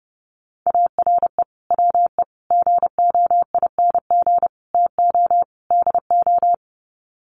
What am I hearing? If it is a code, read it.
Morse code, heard as a keyed tone, about 30 wpm, 700 Hz. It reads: ARE WE GOING TO DO
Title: Sets of 5 Words 30wpm